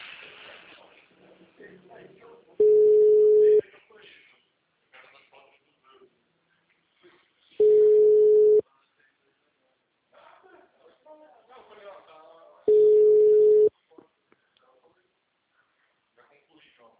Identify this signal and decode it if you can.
Ringback tone (Europe, 425 Hz), with speech.